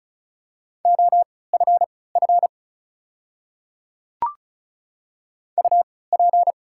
Morse code, heard as a keyed tone, about 36 wpm, spaced as for 20 wpm, 700 Hz. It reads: OFF UP
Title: 3x Character Spacing - Top 200 Words - 35wpm